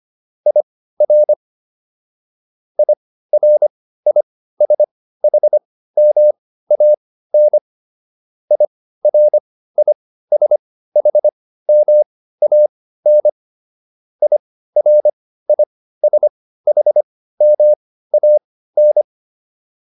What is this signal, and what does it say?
Morse code, heard as a keyed tone, about 25 wpm, spaced as for 15 wpm, 600 Hz. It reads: IR IRISHMAN IRISHMAN IRISHMAN